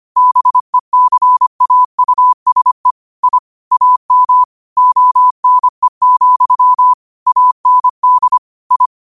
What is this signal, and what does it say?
Morse code, heard as a keyed tone, about 25 wpm, 1000 Hz. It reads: BECAUSE I AM ONE, AND I